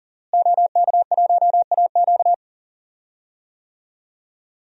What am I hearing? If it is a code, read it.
Morse code, heard as a keyed tone, about 40 wpm, 700 Hz. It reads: OK1AQ
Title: DXCC-100 Call Signs - 5 Characters - Repeated 2x - 40wpm